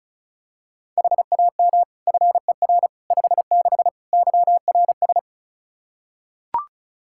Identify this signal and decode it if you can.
Morse code, heard as a keyed tone, about 36 wpm, 700 Hz. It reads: HAM FER 56 YRS
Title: K1USN SST QSO Practice 35wpm